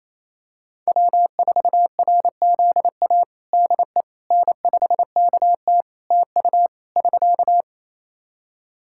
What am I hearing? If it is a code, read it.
Morse code, heard as a keyed tone, about 28 wpm, 700 Hz. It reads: W4RZA DE N5KT TU <SK>
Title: Practice QSO - Exchange 4 - Ending - 28wpm